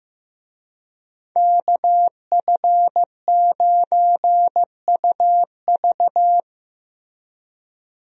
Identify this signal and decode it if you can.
Morse code, heard as a keyed tone, about 15 wpm, 700 Hz. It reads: KF9UV
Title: USA Call Signs - 5 Characters - Repeated 3x - 15wpm